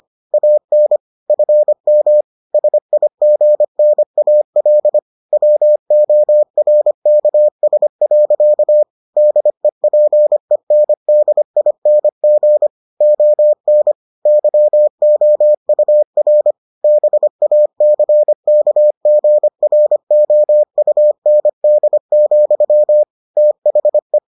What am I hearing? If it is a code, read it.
Morse code, heard as a keyed tone, about 25 wpm, 600 Hz. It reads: AN FM SIGNAL WORKS. DEPENDING ON YOUR BACKGROUND, THE